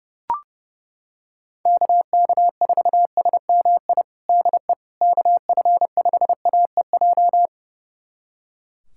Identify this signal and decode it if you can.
Morse code, heard as a keyed tone, about 32 wpm, 700 Hz. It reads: KK4SMI DE KF5AEJ
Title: USA Callsign DE Callsign - 6 Characters - 30wpm